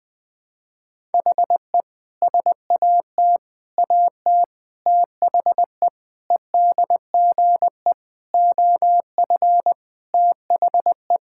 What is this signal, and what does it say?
Morse code, heard as a keyed tone, about 19 wpm, 700 Hz. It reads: HE SAT AT THE EDGE OF THE
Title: Sentences with 5-Word Phrases - 20wpm